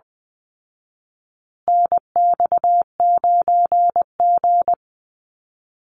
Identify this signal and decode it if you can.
Morse code, heard as a keyed tone, about 20 wpm, 700 Hz. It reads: NX9G
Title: Call Signs Repeated 3x - 20wpm